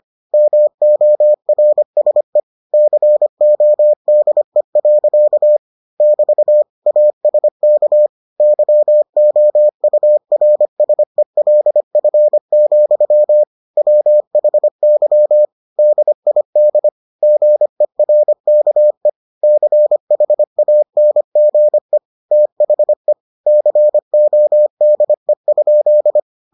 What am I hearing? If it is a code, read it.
Morse code, heard as a keyed tone, about 25 wpm, 600 Hz. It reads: MORSE CODE. <BT> ASK YOURSELF, WHY DID GERKE CHANGE THE CODE?